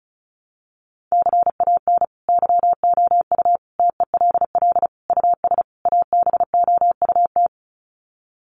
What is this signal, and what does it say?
Morse code, heard as a keyed tone, about 34 wpm, 700 Hz. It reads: CAN YOU TELL US ABOUT
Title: Sets of 5 Words Encore2 35wpm